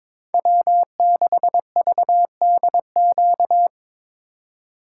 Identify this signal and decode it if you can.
Morse code, heard as a keyed tone, about 26 wpm, 700 Hz. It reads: W6VDQ
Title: Speed Racing - USA 1x3 Call Signs 22wpm